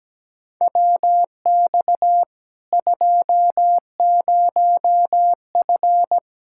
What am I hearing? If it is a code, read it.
Morse code, heard as a keyed tone, about 17 wpm, 700 Hz. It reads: WX 20F
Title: Practice QSO - Exchange 2 - Info - 17wpm